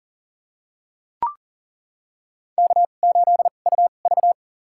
Morse code, heard as a keyed tone, about 40 wpm, spaced as for 29 wpm, 690 Hz. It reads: K8UV
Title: USA Call Signs - 2x Character Spacing - 40wpm